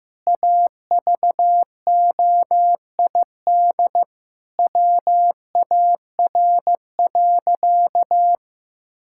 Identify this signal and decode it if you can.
Morse code, heard as a keyed tone, about 15 wpm, 700 Hz. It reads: AVOID WAR.
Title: Sentences from Top 1000 Words - Review Part 6 - 15wpm